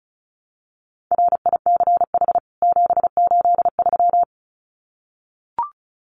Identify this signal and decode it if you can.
Morse code, heard as a keyed tone, about 34 wpm, 700 Hz. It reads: RICH 783
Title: CWT Exchange 35wpm